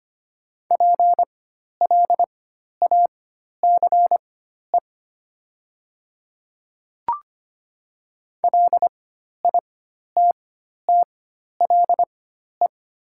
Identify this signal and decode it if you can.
Morse code, heard as a keyed tone, about 25 wpm, spaced as for 12 wpm, 700 Hz. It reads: PLACE LITTLE
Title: 4x Character Spacing - Top 200 Words - 25wpm